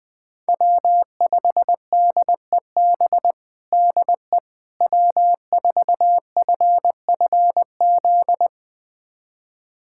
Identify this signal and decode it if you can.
Morse code, heard as a keyed tone, about 20 wpm, 700 Hz. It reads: W5DEB DE W4FFZ
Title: USA Callsign DE Callsign - 5 Characters - 20wpm